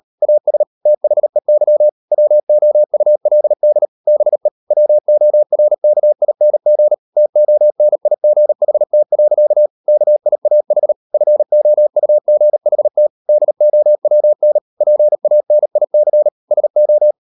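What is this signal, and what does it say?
Morse code, heard as a keyed tone, about 38 wpm, 600 Hz. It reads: AS THEY WOULD BE WORKING TONIGHT. KIAH FOUGHT DOWN PANIC SO
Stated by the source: Morse code practice at 38 WPM, sending Ken Follett's novel Never in English, from CW med Gnister.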